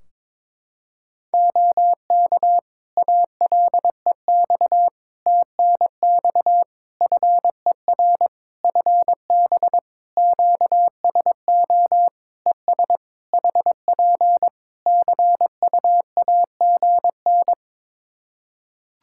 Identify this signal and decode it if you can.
Morse code, heard as a keyed tone, about 22 wpm, 700 Hz. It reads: OK ALEX TNX FER FB QSO ES HP CUAGN
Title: Practice QSO - Exchange 4 - Ending - 22wpm